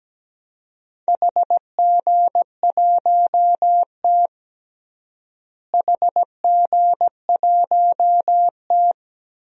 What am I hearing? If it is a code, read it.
Morse code, heard as a keyed tone, about 17 wpm, 700 Hz. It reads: HG1T HG1T
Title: DXCC-100 Call Signs - 4 Characters - Repeated 3x - 17wpm